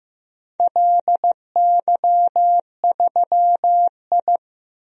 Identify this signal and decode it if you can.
Morse code, heard as a keyed tone, about 16 wpm, 700 Hz. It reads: LY3I
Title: DXCC-100 Call Signs - 4 Characters - 15wpm